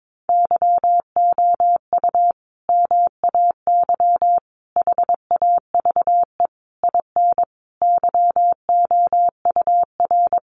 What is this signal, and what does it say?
Morse code, heard as a keyed tone, about 22 wpm, 700 Hz. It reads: YOU MAY HAVE IN YOUR
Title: Sentences from Top 1400 Words 22wpm